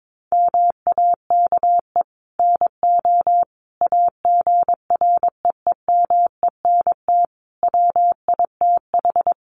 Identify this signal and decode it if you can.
Morse code, heard as a keyed tone, about 22 wpm, 700 Hz. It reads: MAKE NO AGREEMENT WITH
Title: Sentences from Top 1500 Words 22wpm